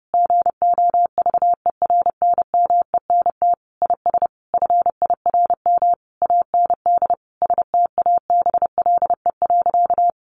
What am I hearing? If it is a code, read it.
Morse code, heard as a keyed tone, about 30 wpm, 700 Hz. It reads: GOVERNMENT IS FIRM AND STABLE.
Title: Sentences from Top 1500 Words 30wpm